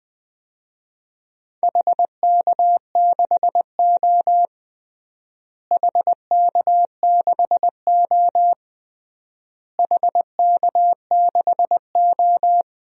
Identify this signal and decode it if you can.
Morse code, heard as a keyed tone, about 21 wpm, 700 Hz. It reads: HK6O HK6O HK6O
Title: DXCC-100 Call Signs - 4 Characters - Repeated 3x - 20wpm